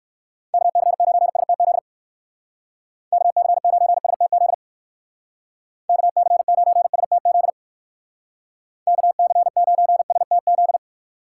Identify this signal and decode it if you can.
Morse code, heard as a keyed tone, about 59 wpm, 700 Hz. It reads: KC9STZ KC9STZ KC9STZ KC9STZ